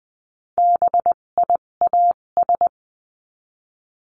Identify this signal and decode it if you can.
Morse code, heard as a keyed tone, about 20 wpm, spaced as for 17 wpm, 700 Hz. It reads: BIAS
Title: 257 Lesson 42b - Words 14-20wpm